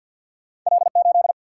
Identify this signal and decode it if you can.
Morse code, heard as a keyed tone, about 50 wpm, 700 Hz. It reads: L 8
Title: Single-Letter Single-Number 50wpm